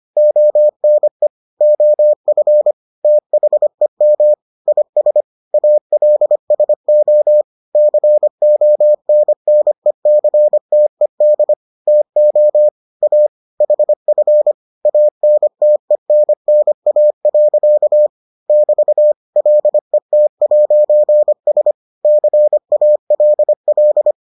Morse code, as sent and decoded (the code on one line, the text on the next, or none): --- -. / --- ..-. / - .... -- / .. ... / .- .-.. ... --- / -.-. --- -. -. -.-. - . -.. / - --- / .- / .... ..-. / .- -. - . -. -. .- .-.-.- / -...- / .-.. - .----. ... / -.-. .- .-.. .-..
ONE OF THEM IS ALSO CONNECTED TO A HF ANTENNA. <BT> LET'S CALL